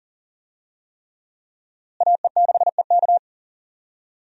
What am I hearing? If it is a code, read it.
Morse code, heard as a keyed tone, about 40 wpm, 700 Hz. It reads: AE6EK
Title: USA Call Signs - 5 Characters - Repeated 3x - 40wpm